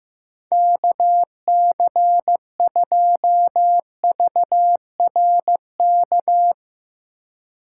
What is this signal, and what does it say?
Morse code, heard as a keyed tone, about 15 wpm, 700 Hz. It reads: KC2VRK
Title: Speed Racing - USA 2x3 Call Signs 15wpm